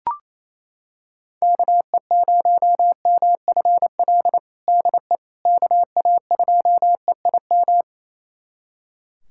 Morse code, heard as a keyed tone, about 28 wpm, 700 Hz. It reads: KE0MFL DE KA2EIM
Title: USA Callsign DE Callsign - 6 Characters - 28wpm